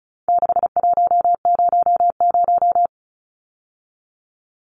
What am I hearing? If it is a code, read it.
Morse code, heard as a keyed tone, about 34 wpm, 700 Hz. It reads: 6100
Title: Numbers - 4-Digit 35wpm